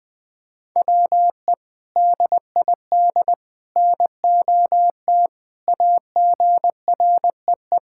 Morse code, heard as a keyed tone, about 21 wpm, 700 Hz. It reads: WE DID NOT AGREE
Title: Sentences with 3-Word Phrases - 20wpm